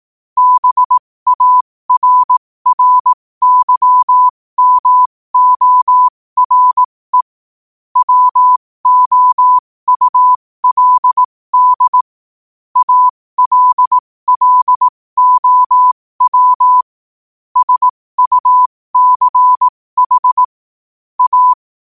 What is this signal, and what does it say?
Morse code, heard as a keyed tone, about 18 wpm, spaced as for 15 wpm, 1000 Hz. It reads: BARRYMORE WOULD ALLOW SUCH A
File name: qotdfiles / 15wpm